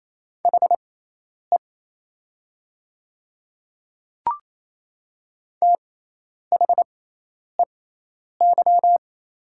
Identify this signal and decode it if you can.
Morse code, heard as a keyed tone, about 27 wpm, spaced as for 10 wpm, 700 Hz. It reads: HE THEY
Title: 6x Character Spacing - Top 200 Words - 28wpm